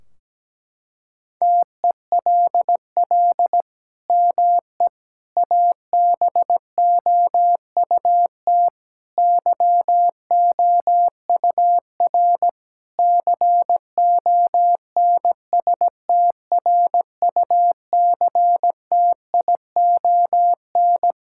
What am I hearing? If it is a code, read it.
Morse code, heard as a keyed tone, about 17 wpm, 700 Hz. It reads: TELL ME ABOUT YOUR CONSTRUCTION
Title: Sentences from Top 1600 Words 17wpm